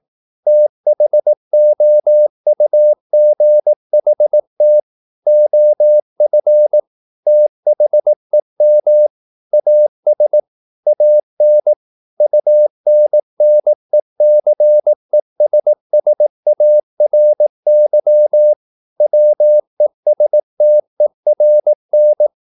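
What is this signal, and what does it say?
Morse code, heard as a keyed tone, about 18 wpm, 600 Hz. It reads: THOUGHT OF THEM AS AN UNNECESSARY WESTERN
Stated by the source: Morse practice at 18 wpm, English text, from CW med Gnister.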